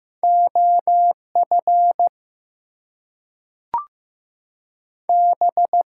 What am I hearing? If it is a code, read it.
Morse code, heard as a keyed tone, about 15 wpm, 700 Hz. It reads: OF B
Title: Two Letter Words - 15wpm